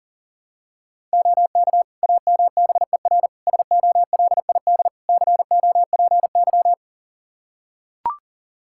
Morse code, heard as a keyed tone, about 40 wpm, 700 Hz. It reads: OK AMBER SOLID COPY
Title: Practice QSO - Exchange 3 - More Info - 40wpm